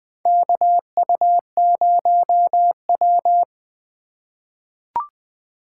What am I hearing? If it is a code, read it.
Morse code, heard as a keyed tone, about 20 wpm, 700 Hz. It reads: KU0W
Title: USA Call Signs - 4 Characters - 3x Character Spacing - 20wpm